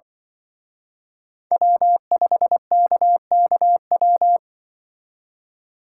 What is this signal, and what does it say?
Morse code, heard as a keyed tone, about 25 wpm, 700 Hz. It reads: W5KKW
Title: Speed Racing - USA 1x3 Call Signs 20wpm